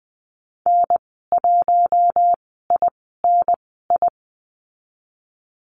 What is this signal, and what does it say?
Morse code, heard as a keyed tone, about 20 wpm, spaced as for 14 wpm, 700 Hz. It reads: N1INI
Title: 048 Lesson 9g - Number 4 - Call Signs 20wpm